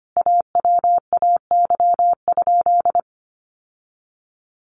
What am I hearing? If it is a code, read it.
Morse code, heard as a keyed tone, about 25 wpm, 700 Hz. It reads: AWAY?
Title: Sentences from Top 600 Words 25wpm